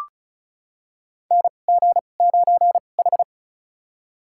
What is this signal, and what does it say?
Morse code, heard as a keyed tone, about 36 wpm, spaced as for 25 wpm, 700 Hz. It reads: NG9H
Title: USA Call Signs - 4 Characters - 2x Character Spacing - 35wpm